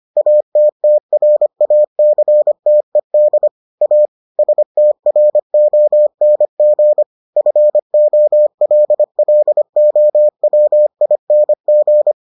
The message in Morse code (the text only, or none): ATTRACTED A STRONG FOLLOWING